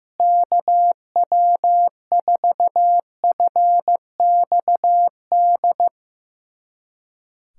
Morse code, heard as a keyed tone, about 16 wpm, 700 Hz. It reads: KW4FXD
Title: USA Callsign DE Callsign - 6 Characters - 15wpm